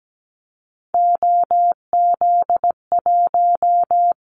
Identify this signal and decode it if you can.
Morse code, heard as a keyed tone, about 17 wpm, 700 Hz. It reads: OZ1
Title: Callsign Prefixes for Top 50 DXCC - 17wpm